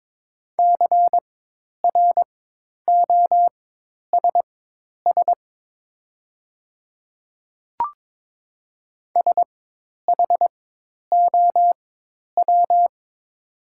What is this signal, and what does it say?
Morse code, heard as a keyed tone, about 22 wpm, spaced as for 10 wpm, 700 Hz. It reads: CROSS SHOW
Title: 4x Character Spacing - Top 500 Words - 22wpm